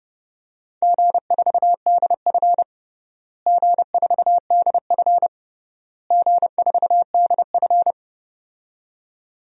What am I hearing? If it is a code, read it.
Morse code, heard as a keyed tone, about 30 wpm, 700 Hz. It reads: G4DF G4DF G4DF
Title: DXCC-100 Call Signs - 4 Characters - Repeated 3x - 30wpm